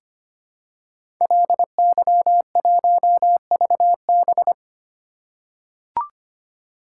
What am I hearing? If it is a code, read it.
Morse code, heard as a keyed tone, about 25 wpm, 700 Hz. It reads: LY1VB
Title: Code Groups - 5 Letter:Numbers - 25wpm